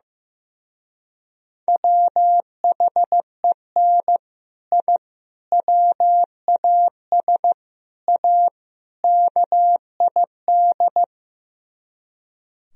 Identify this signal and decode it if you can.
Morse code, heard as a keyed tone, about 15 wpm, 700 Hz. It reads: WHEN I WAS A KID
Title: Speed-Racing - Sets of 5 Words 15wpm